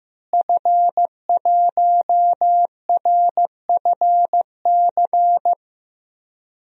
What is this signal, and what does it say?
Morse code, heard as a keyed tone, about 15 wpm, 700 Hz. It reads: F1RFC
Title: DXCC-100 Call Signs - 5 Characters - Repeated 3x - 15wpm